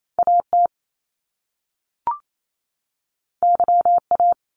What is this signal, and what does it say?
Morse code, heard as a keyed tone, about 28 wpm, 700 Hz. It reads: AT YA
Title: Top Two Letter Combinations in Words 28wpm